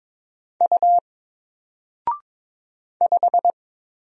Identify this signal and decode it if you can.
Morse code, heard as a keyed tone, about 22 wpm, 700 Hz. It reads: U 5
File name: Single-Letter-Number-22wpm